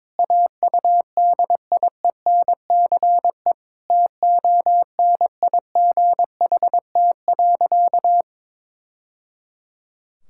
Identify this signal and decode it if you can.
Morse code, heard as a keyed tone, about 21 wpm, 700 Hz. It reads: AUDIENCE TONIGHT.
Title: Sentences from Top 1000 Words - Review Part 5 - 22wpm